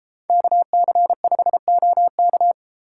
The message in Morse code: KC5OK

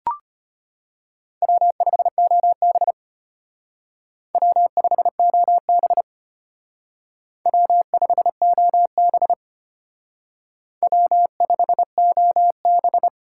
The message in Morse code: W5OB W5OB W5OB W5OB

